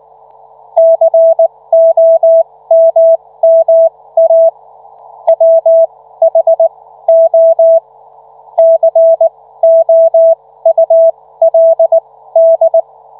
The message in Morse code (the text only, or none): COMMA WHO COULD